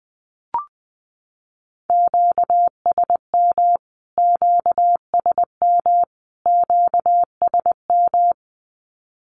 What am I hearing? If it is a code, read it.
Morse code, heard as a keyed tone, about 20 wpm, 700 Hz. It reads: QSM QSM QSM
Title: Q-Codes Repeated 3x 20wpm